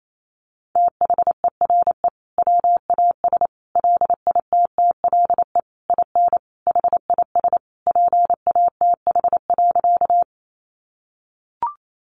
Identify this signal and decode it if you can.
Morse code, heard as a keyed tone, about 28 wpm, 700 Hz. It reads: THERE WAS LITTLE IN HIS PATH.
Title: Sentences from Top 1400 Words 28wpm